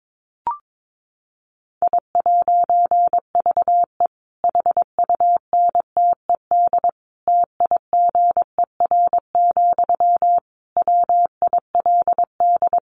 Morse code, heard as a keyed tone, about 22 wpm, 700 Hz. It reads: I'VE HUNTED TIGER, WILD
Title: A Sound Of Thunder 22wpm